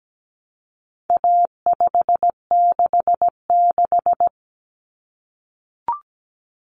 Morse code, heard as a keyed tone, about 17 wpm, 700 Hz. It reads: A566
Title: Common Cut Numbers - 4-Digit 17wpm